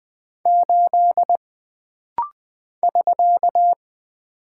Morse code, heard as a keyed tone, about 20 wpm, 700 Hz. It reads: 8 <SK>
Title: Single Character (Letter:Number:Punctuation:Prosign) - Rapid-Fire - 20wpm